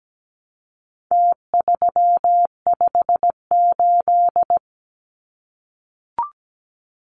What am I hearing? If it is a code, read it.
Morse code, heard as a keyed tone, about 17 wpm, 700 Hz. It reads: T358
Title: Numbers - Contest Style - 4-Digit 17wpm